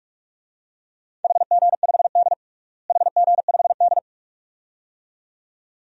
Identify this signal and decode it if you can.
Morse code, heard as a keyed tone, about 46 wpm, 700 Hz. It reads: HG5D HG5D
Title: DXCC-100 Call Signs - 4 Characters - Repeated 2x - 45wpm